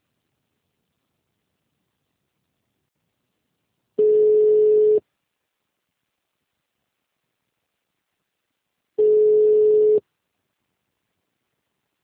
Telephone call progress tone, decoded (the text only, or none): ringback tone